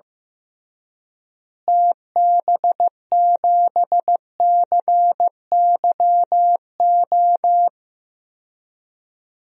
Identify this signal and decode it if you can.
Morse code, heard as a keyed tone, about 15 wpm, 700 Hz. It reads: TB7CYO
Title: DXCC-100 Call Signs - 6 Characters - Repeated 3x - 15wpm